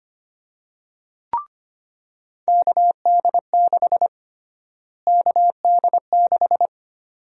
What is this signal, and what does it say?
Morse code, heard as a keyed tone, about 25 wpm, 700 Hz. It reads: KD6 KD6
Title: USA Call Sign Prefixes - Repeated 2x - 25wpm